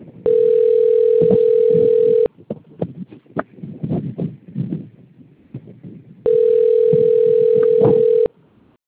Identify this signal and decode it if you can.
Ringback tone (North America), 440 + 480 Hz.
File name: rg-1600-unknown- - - .10011